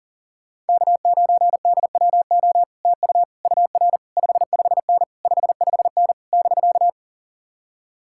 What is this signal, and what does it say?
Morse code, heard as a keyed tone, about 52 wpm, 700 Hz. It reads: K9DWO TU UR 55N 55N <BK>